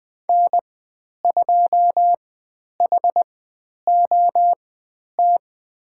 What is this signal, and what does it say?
Morse code, heard as a keyed tone, about 20 wpm, spaced as for 10 wpm, 700 Hz. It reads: N2HOT
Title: 109 Lesson 18g - Letter m - Call Signs 20wpm